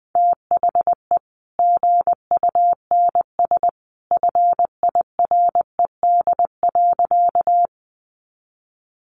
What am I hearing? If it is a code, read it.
Morse code, heard as a keyed tone, about 20 wpm, 700 Hz. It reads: THE GUNS FIRED.
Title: A Sound Of Thunder 20wpm